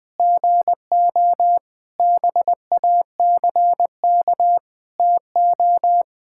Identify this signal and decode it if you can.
Morse code, heard as a keyed tone, about 20 wpm, 700 Hz. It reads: GO BACK TO
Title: Sentences with 7-Word Phrases - Spoken Beginning - 20wpm